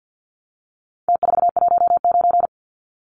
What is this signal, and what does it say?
Morse code, heard as a keyed tone, about 50 wpm, 700 Hz. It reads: T419
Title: Numbers - Contest Style - 4-Digit 50wpm